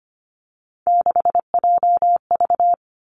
Morse code, heard as a keyed tone, about 25 wpm, 700 Hz. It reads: 6JV
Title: Three Challenging Letter-Number Combinations 25wpm